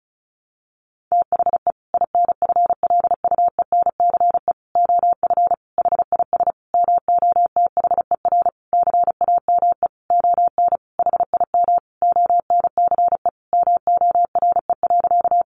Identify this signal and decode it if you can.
Morse code, heard as a keyed tone, about 34 wpm, 700 Hz. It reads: THE INFLUENCE OF HIS MOTHER CAME ON HIM ONCE MORE.
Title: Sentences from Top 1100 Words 35wpm